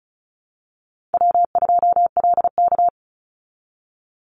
Morse code, heard as a keyed tone, about 34 wpm, 700 Hz. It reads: W2LK
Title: Easy Call Signs Repeated 3x - 35wpm